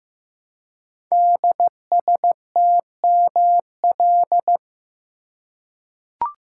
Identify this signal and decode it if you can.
Morse code, heard as a keyed tone, about 15 wpm, 700 Hz. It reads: DSTML
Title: Code Groups - 5 Letters - 15wpm